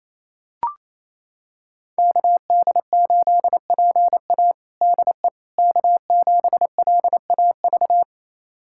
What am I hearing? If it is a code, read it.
Morse code, heard as a keyed tone, about 28 wpm, 700 Hz. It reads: KD8PA DE K7LAV
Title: USA Callsign DE Callsign - 5 Characters - 28wpm